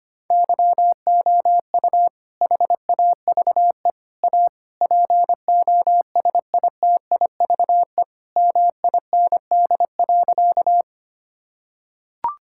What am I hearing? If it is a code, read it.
Morse code, heard as a keyed tone, about 25 wpm, 700 Hz. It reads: YOU HAVE A POSITIVE MIND.
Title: Sentences from Top 1000 Words - Review Part 5 - 25wpm